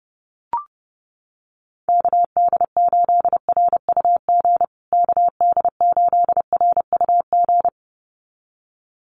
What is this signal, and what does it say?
Morse code, heard as a keyed tone, about 30 wpm, 700 Hz. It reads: KD8RUG KD8RUG
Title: USA Call Signs Repeated 2x 30wpm